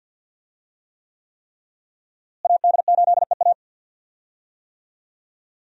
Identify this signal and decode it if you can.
Morse code, heard as a keyed tone, about 45 wpm, 700 Hz. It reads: AD8EA
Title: USA Call Signs - 5 Characters - Repeated 3x - 50wpm